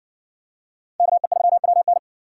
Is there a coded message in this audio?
Yes: Morse code from 1.0 to 2.0 s, heard as a keyed tone, about 57 wpm, 700 Hz. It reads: KE3WN